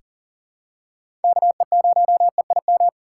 Morse code, heard as a keyed tone, about 40 wpm, 700 Hz. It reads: KE0EIM
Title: USA Call Signs - 6 Characters - 3x Character Spacing - 40wpm